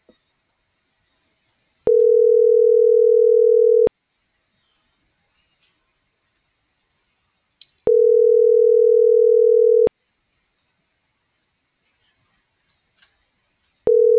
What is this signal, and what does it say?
Ringback tone (North America), 440 + 480 Hz.